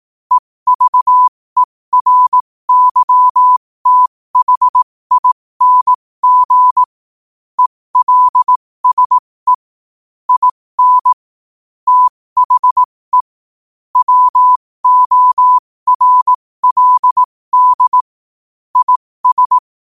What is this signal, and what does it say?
Morse code, heard as a keyed tone, about 18 wpm, spaced as for 15 wpm, 1000 Hz. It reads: EVERYTHING ELSE IN THE WORLD IS
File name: qotdfiles / 15wpm